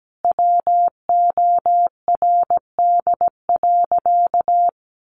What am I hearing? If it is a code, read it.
Morse code, heard as a keyed tone, about 17 wpm, 700 Hz. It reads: WORD.
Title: Sentences from Top 1100 Words 17wpm